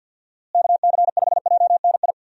Morse code, heard as a keyed tone, about 51 wpm, 700 Hz. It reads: KC5JNI
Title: USA Call Signs - 6 Characters - Repeated 3x - 50wpm